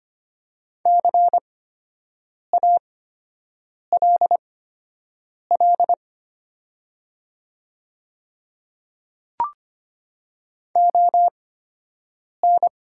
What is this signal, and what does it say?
Morse code, heard as a keyed tone, about 24 wpm, spaced as for 7 wpm, 700 Hz. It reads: CALL ON
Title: 8x Character Spacing - Top 100 Words - 25wpm